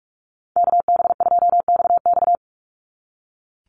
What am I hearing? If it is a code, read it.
Morse code, heard as a keyed tone, about 44 wpm, 700 Hz. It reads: KB2XX
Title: NA SOTA Chaser Callsigns 45wpm